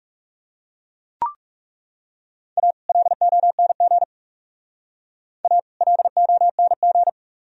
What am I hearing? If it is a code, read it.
Morse code, heard as a keyed tone, about 37 wpm, 700 Hz. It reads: A LONG A LONG
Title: Speed-Racing - Sets of 2 Words 30wpm